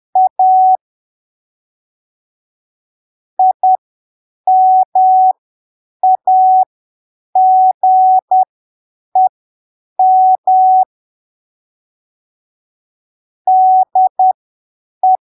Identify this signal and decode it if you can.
Morse code, heard as a keyed tone, about 10 wpm, spaced as for 7 wpm, 750 Hz. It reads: A IMAGEM DE
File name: arquivos / texto / texto 26 10ppm